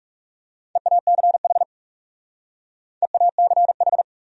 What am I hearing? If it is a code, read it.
Morse code, heard as a keyed tone, about 37 wpm, 700 Hz. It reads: EACH EACH
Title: Speed-Racing Top 100 Words 30wpm